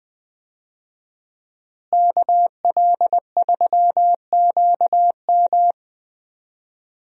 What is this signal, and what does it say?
Morse code, heard as a keyed tone, about 20 wpm, 700 Hz. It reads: KL3QM
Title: USA Call Signs - 5 Characters - Repeated 2x - 20wpm